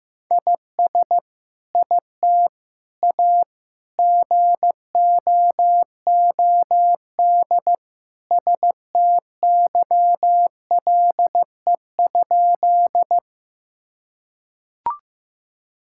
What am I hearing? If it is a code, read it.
Morse code, heard as a keyed tone, about 15 wpm, 700 Hz. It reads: IS IT A GOOD STYLE?
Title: Sentences from Top 1000 Words - Review Part 5 - 15wpm